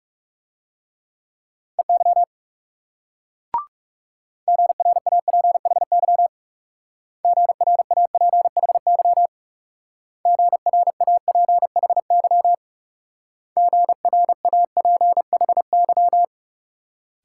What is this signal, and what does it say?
Morse code, heard as a keyed tone, about 38 wpm, 700 Hz. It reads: EY GRAPHY GRAPHY GRAPHY GRAPHY
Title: Speed-Racing Multi-letter Phonemes - 30wpm